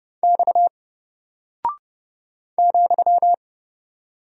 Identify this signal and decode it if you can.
Morse code, heard as a keyed tone, about 30 wpm, 700 Hz. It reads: X ,
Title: Single Character (Letter:Number:Punctuation:Prosign) 30wpm